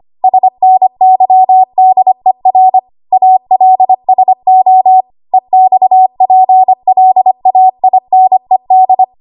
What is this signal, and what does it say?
Morse code, heard as a keyed tone, about 25 wpm, 750 Hz. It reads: SNYDER ALSO EXPLAINED
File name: Morse / 25WPM